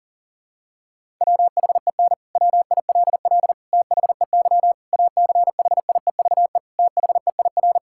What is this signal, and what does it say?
Morse code, heard as a keyed tone, about 40 wpm, 700 Hz. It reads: WHEN WILL THEY ACHIEVE THEIR
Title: Sentences from Top 1000 Words - Review Part 6 - 40wpm